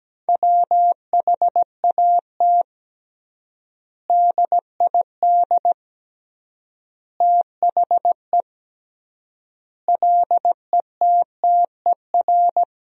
Morse code, heard as a keyed tone, about 17 wpm, 700 Hz. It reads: WHAT DID THE LETTER
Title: 3x Word-Spacing - Sentences from Top 500 Words - 17wpm